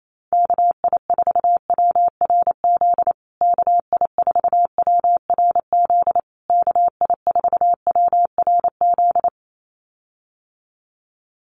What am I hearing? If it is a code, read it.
Morse code, heard as a keyed tone, about 28 wpm, 700 Hz. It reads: KI4WRZ KI4WRZ KI4WRZ
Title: USA Call Signs Repeated 3x 28wpm